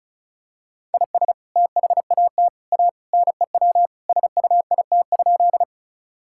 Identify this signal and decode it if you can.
Morse code, heard as a keyed tone, about 36 wpm, 700 Hz. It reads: IS THAT A NEW SUIT?
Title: Sentences from Top 1600 Words 35wpm